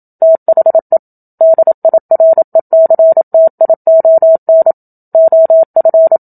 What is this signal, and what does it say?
Morse code, heard as a keyed tone, about 27 wpm, 650 Hz. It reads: THE DIRECTION OF
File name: Book Two / 01-Chapter 1 - 27 WPM